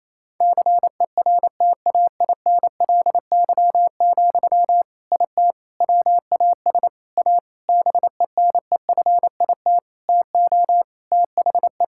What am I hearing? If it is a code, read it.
Morse code, heard as a keyed tone, about 28 wpm, 700 Hz. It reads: CERTAINLY, IT WAS A BENEFIT TO THE
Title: Sentences from Top 1000 Words - Review Part 6 - 28wpm